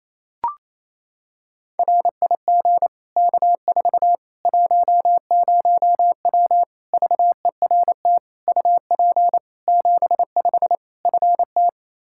Morse code, heard as a keyed tone, about 28 wpm, 700 Hz. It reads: RIG K4 10W VERT UP 75 FT
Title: Practice QSO - Exchange 2 - Info - 28wpm